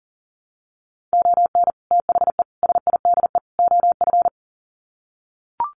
Morse code, heard as a keyed tone, about 40 wpm, 700 Hz. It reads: ON THE SIDE OF
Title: Sets of 4 Words 40wpm